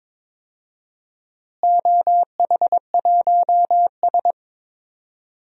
Morse code, heard as a keyed tone, about 22 wpm, 700 Hz. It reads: OH1S
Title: DXCC-100 Call Signs - 4 Characters - Repeated 3x - 22wpm